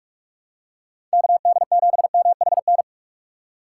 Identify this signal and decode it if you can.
Morse code, heard as a keyed tone, about 42 wpm, 700 Hz. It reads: KD7MHN
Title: USA Call Signs - 6 Characters - 4x Character Spacing - 45wpm